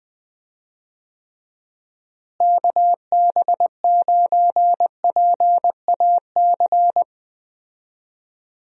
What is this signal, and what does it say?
Morse code, heard as a keyed tone, about 20 wpm, 700 Hz. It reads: KB9PAC
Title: USA Call Signs - 6 Characters - Repeated 2x - 20wpm